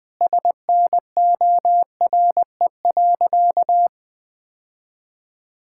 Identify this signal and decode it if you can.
Morse code, heard as a keyed tone, about 20 wpm, 700 Hz. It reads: SNORE.
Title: All-Age Appropriate Jokes - 20wpm